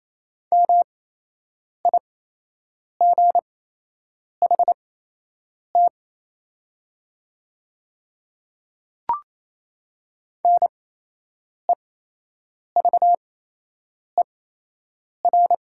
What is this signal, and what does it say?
Morse code, heard as a keyed tone, about 28 wpm, spaced as for 8 wpm, 700 Hz. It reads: MIGHT NEVER
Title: 8x Character Spacing - Top 500 Words - 28wpm